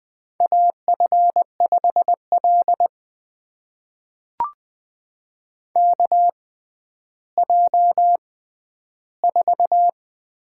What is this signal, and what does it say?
Morse code, heard as a keyed tone, about 20 wpm, spaced as for 11 wpm, 700 Hz. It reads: AF5L KJ4
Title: USA Call Signs - 4 Characters - 6x Character Spacing - 20wpm